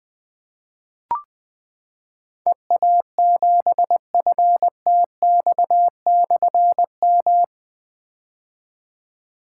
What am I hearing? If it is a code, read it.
Morse code, heard as a keyed tone, about 20 wpm, 700 Hz. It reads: EA7FTX/M
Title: Difficult Call Signs 20wpm